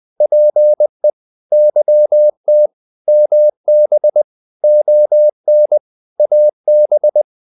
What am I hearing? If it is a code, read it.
Morse code, heard as a keyed tone, about 20 wpm, 600 Hz. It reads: PE YT MB ON AB
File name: Provinces-20